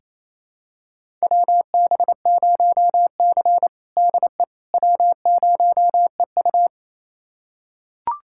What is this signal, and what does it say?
Morse code, heard as a keyed tone, about 28 wpm, 700 Hz. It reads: WB0C DE W0EU
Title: USA Callsign DE Callsign - 4 Characters - 28wpm